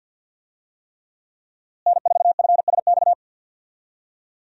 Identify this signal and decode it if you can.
Morse code, heard as a keyed tone, about 52 wpm, 700 Hz. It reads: N4FSX
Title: USA Call Signs - 5 Characters - Repeated 3x - 50wpm